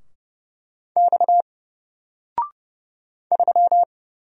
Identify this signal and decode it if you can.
Morse code, heard as a keyed tone, about 31 wpm, 700 Hz. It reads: X 3
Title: Single Letter-Number Alt 30wpm